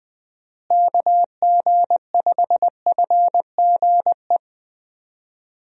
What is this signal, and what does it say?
Morse code, heard as a keyed tone, about 21 wpm, 700 Hz. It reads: KG5FGE
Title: USA Call Signs - 6 Characters - Repeated 2x - 20wpm